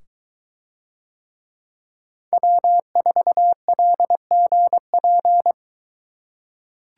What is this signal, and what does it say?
Morse code, heard as a keyed tone, about 22 wpm, 700 Hz. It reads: W4LGP